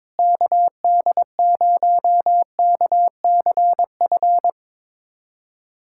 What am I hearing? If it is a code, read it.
Morse code, heard as a keyed tone, about 22 wpm, 700 Hz. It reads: KD0KCF
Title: USA Callsign DE Callsign - 6 Characters - 22wpm